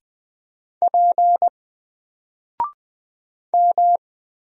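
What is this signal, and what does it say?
Morse code, heard as a keyed tone, about 20 wpm, 700 Hz. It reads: P M
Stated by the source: Single letter and number Morse drill at 20 wpm, from N4LCD.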